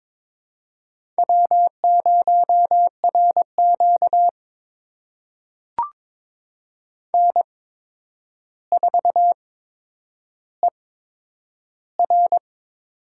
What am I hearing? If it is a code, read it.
Morse code, heard as a keyed tone, about 22 wpm, spaced as for 9 wpm, 700 Hz. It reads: W0RQ N4ER
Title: USA Call Signs - 4 Characters - 8x Character Spacing - 22wpm